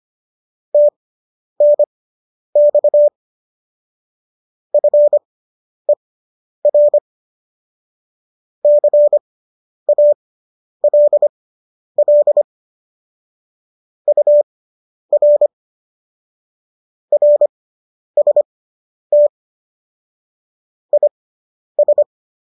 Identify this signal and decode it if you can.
Morse code, heard as a keyed tone, about 25 wpm, spaced as for 10 wpm, 590 Hz. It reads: TNX FER CALL UR RST IS